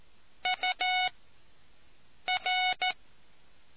Morse code, heard as a keyed tone, about 14 wpm, spaced as for 6 wpm, 740 Hz. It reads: UR